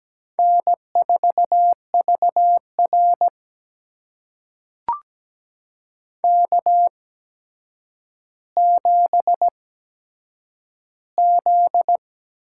Morse code, heard as a keyed tone, about 17 wpm, spaced as for 8 wpm, 700 Hz. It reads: N4VR K7Z
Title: USA Call Signs - 8x Character Spacing - 17wpm